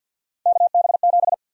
Morse code, heard as a keyed tone, about 52 wpm, 700 Hz. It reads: KB7
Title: USA Call Sign Prefixes - 50wpm